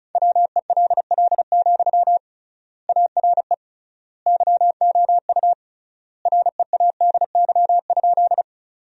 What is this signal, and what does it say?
Morse code, heard as a keyed tone, about 36 wpm, 700 Hz. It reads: WELL, ARE YOU READY?
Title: 3x Word-Spacing - Sentences from Top 400 Words - 35wpm